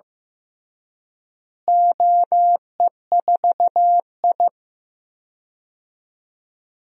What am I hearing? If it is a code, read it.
Morse code, heard as a keyed tone, about 15 wpm, 700 Hz. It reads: OE4I
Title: DXCC-100 Call Signs - 4 Characters - Repeated 3x - 15wpm